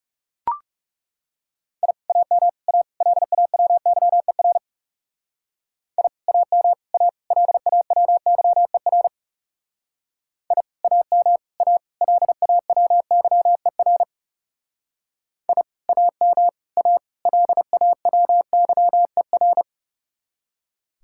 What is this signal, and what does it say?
Morse code, heard as a keyed tone, about 37 wpm, 700 Hz. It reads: I AM A LAWYER I AM A LAWYER I AM A LAWYER I AM A LAWYER
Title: Speed-Racing Common QSO Phrases - 30wpm